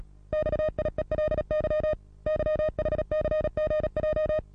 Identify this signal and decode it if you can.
Morse code, heard as a keyed tone, about 37 wpm, 620 Hz. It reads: XIELY YHCGJ
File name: repos / Morsekurs / DJ4UF Morsen27